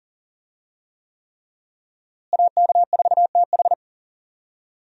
Morse code, heard as a keyed tone, about 40 wpm, 700 Hz. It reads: AK4TH
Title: USA Call Signs - 5 Characters - Repeated 3x - 40wpm